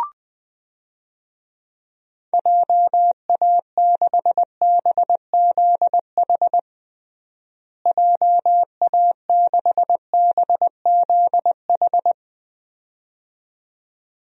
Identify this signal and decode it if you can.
Morse code, heard as a keyed tone, about 20 wpm, 700 Hz. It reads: JA6BZH JA6BZH
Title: DXCC-100 Call Signs - 6 Characters - Repeated 2x - 20wpm